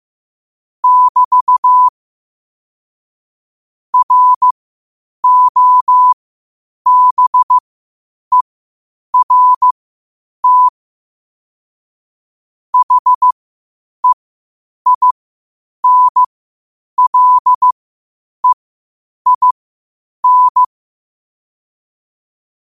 Morse code, heard as a keyed tone, about 15 wpm, spaced as for 8 wpm, 1000 Hz. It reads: <BT> ROBERT HEINLEIN
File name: qotdfiles / 8wpm